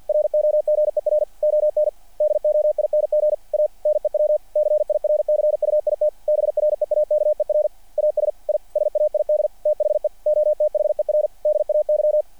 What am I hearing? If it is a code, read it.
Morse code, heard as a keyed tone, about 48 wpm, 600 Hz. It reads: ROGER ON DOING A NEW CIRCUIT BREAKER AS I SAID THE OTHER DAY
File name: archived / CRSnet